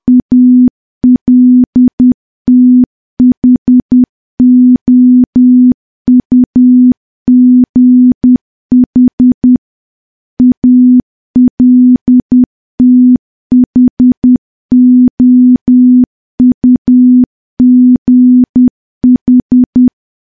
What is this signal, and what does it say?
Morse code, heard as a keyed tone, about 10 wpm, 250 Hz. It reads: ALTHOUGH ALTHOUGH
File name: morse / morse-training-10WPM